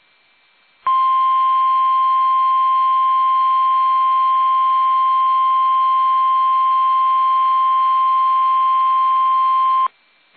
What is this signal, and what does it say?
Emergency Alert System (SAME) tones (the 1050 Hz attention tone). It is heard from 0.8 to 9.9 s.
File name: eas / KUSO / KUSO